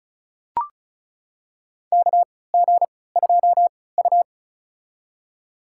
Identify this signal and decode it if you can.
Morse code, heard as a keyed tone, about 34 wpm, spaced as for 20 wpm, 700 Hz. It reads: KG2U
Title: USA Call Signs - 4 Characters - 3x Character Spacing - 35wpm